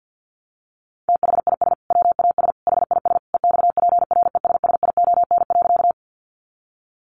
Morse code, heard as a keyed tone, about 48 wpm, 700 Hz. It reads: THIS WAS HIS EXPRESSION.
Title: Sentences from Top 1200 Words 50wpm